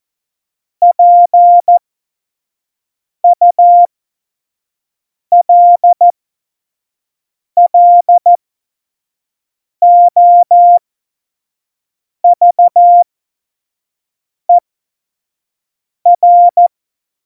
Morse code, heard as a keyed tone, about 14 wpm, spaced as for 5 wpm, 700 Hz. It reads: PULLOVER
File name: Morsekurs / Witze / Witz 04. Februar 2020, 25 F70